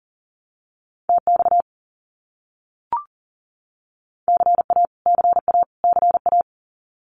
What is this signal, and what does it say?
Morse code, heard as a keyed tone, about 40 wpm, 700 Hz. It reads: TX CA CA CA
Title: US States Repeated 3x 40wpm